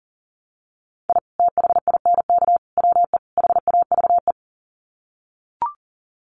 Morse code, heard as a keyed tone, about 40 wpm, 700 Hz. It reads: I THINK WE HAVE
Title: Sets of 4 Words 40wpm